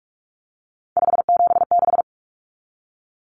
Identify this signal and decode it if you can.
Morse code, heard as a keyed tone, about 44 wpm, 700 Hz. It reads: 576
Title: Common Cut Numbers - 3-Digit 45wpm